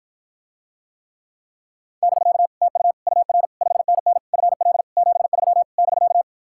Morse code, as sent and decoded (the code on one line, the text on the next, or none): -...-.- / - ..- / ..- .-. / ..... -. -. / ..-. .-.. / --... ...-- / -...-.-
<BK> TU UR 5NN FL 73 <BK>